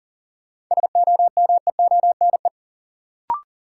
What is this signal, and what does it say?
Morse code, heard as a keyed tone, about 40 wpm, 700 Hz. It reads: SOMEONE
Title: Sentences from Top 600 Words - New Word Drill - 40wpm